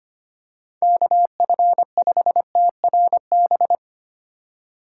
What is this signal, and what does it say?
Morse code, heard as a keyed tone, about 25 wpm, 700 Hz. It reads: KF5TRB
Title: USA Call Signs - 6 Characters - 4x Character Spacing - 25wpm